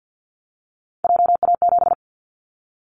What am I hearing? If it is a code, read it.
Morse code, heard as a keyed tone, about 50 wpm, 700 Hz. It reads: JA7
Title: Callsign Prefixes for Top 25 DXCC - 50wpm